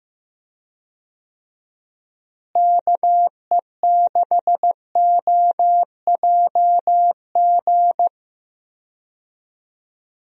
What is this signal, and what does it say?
Morse code, heard as a keyed tone, about 15 wpm, 700 Hz. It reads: KE6OJG
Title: USA Call Signs - 6 Characters - Repeated 3x - 15wpm